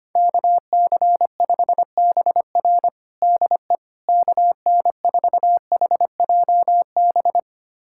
Morse code, heard as a keyed tone, about 25 wpm, 700 Hz. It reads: KC5BR DE KN4HJB
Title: USA Callsign DE Callsign 25wpm